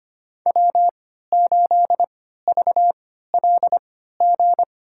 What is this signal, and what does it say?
Morse code, heard as a keyed tone, about 25 wpm, spaced as for 14 wpm, 700 Hz. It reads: W8VLG
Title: USA Call Signs - 3x Character Spacing - 25wpm